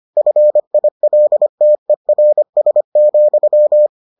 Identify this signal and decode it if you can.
Morse code, heard as a keyed tone, about 25 wpm, 600 Hz. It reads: FILTERS,